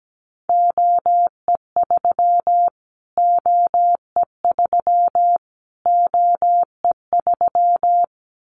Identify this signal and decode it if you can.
Morse code, heard as a keyed tone, about 17 wpm, 700 Hz. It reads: OE3 OE3 OE3
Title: Callsign Prefixes for Top 25 DXCC - Repeated 3x - 17wpm